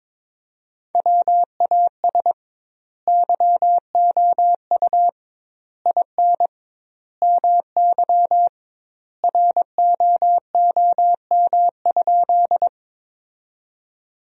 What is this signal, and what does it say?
Morse code, heard as a keyed tone, about 22 wpm, 700 Hz. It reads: WAS YOU IN MY ROOM?
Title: 2x Word-Spacing - Sentences from Top 300 Words - 22wpm